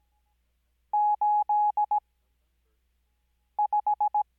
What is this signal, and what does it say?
Morse code, heard as a keyed tone, about 17 wpm, spaced as for 5 wpm, 840 Hz. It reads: 85
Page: Farnsworth Spacing: 15WPM Character speed Spaced at 5WPM.